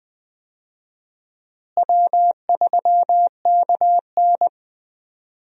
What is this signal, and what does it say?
Morse code, heard as a keyed tone, about 20 wpm, 700 Hz. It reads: W3KN
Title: USA Call Signs - 4 Characters - Repeated 2x - 20wpm